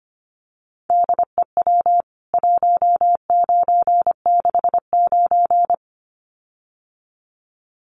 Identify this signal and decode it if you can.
Morse code, heard as a keyed tone, about 25 wpm, 700 Hz. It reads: DEW 1969
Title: CWT Exchange 25wpm